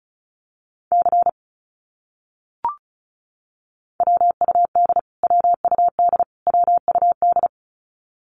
Morse code, heard as a keyed tone, about 34 wpm, 700 Hz. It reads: C WUD WUD WUD
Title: Abbreviations Repeated 3x - 35wpm